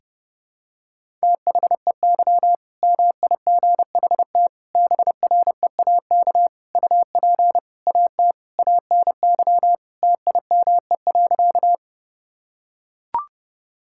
Morse code, heard as a keyed tone, about 30 wpm, 700 Hz. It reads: THEY MIGHT BREAK UP AT ANY TIME.
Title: Sentences from Top 1000 Words - Review Part 6 - 30wpm